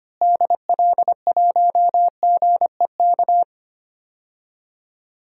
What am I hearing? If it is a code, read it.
Morse code, heard as a keyed tone, about 25 wpm, 700 Hz. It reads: DL1GEK
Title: DXCC-100 Call Signs - 6 Characters - Repeated 2x - 25wpm